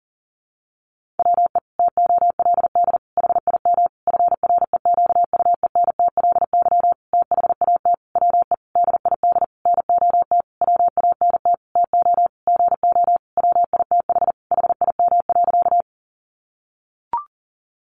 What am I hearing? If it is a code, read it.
Morse code, heard as a keyed tone, about 40 wpm, 700 Hz. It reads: WE TOLD HIM FREQUENTLY THAT WE DID NOT WANT TO GO WITH HIM.
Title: Sentences from Top 1300 Words 40wpm